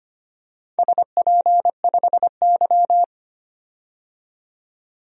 Morse code, heard as a keyed tone, about 25 wpm, 700 Hz. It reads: SP5Y
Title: DXCC-100 Call Signs - 4 Characters - 25wpm